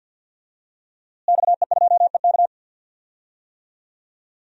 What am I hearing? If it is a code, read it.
Morse code, heard as a keyed tone, about 51 wpm, 700 Hz. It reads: XE2EK
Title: DXCC-100 Call Signs - 5 Characters - Repeated 3x - 50wpm